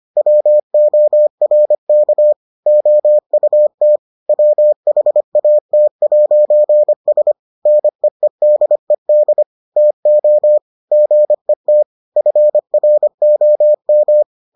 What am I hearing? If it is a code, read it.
Morse code, heard as a keyed tone, about 25 wpm, 600 Hz. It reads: WORK OUT WHAT'S NEEDED TO GET FROM